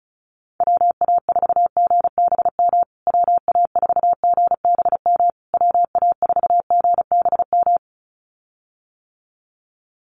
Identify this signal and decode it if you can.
Morse code, heard as a keyed tone, about 34 wpm, 700 Hz. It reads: WA4GBM WA4GBM WA4GBM
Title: USA Call Signs Repeated 3x 35wpm